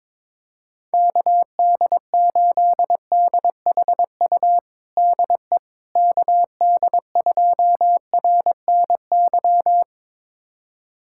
Morse code, heard as a keyed tone, about 22 wpm, 700 Hz. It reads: KD8DHU DE KD2RNY
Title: USA Callsign DE Callsign 22wpm